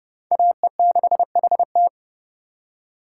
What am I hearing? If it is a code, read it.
Morse code, heard as a keyed tone, about 30 wpm, 700 Hz. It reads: AE6HT
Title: USA Call Signs - 5 Characters - 2x Character Spacing - 30wpm